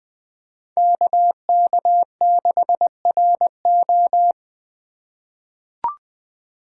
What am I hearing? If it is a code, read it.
Morse code, heard as a keyed tone, about 20 wpm, 700 Hz. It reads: KK6RO
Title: USA Call Signs - 6x Character Spacing - 20wpm